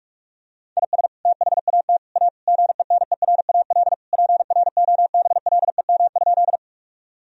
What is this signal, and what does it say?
Morse code, heard as a keyed tone, about 46 wpm, 700 Hz. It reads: IS THAT A GENERAL PROBLEM?
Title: Sentences from Top 1000 Words - Review Part 3 - 45wpm